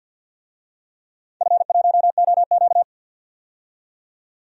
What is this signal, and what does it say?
Morse code, heard as a keyed tone, about 46 wpm, 700 Hz. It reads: F1OQ
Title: DXCC-100 Call Signs - 4 Characters - 50wpm